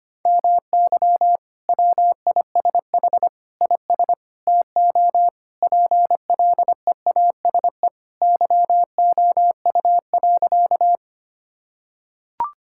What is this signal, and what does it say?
Morse code, heard as a keyed tone, about 25 wpm, 700 Hz. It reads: MY WISH IS TO PLEASE YOU.
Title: Sentences from Top 1600 Words 25wpm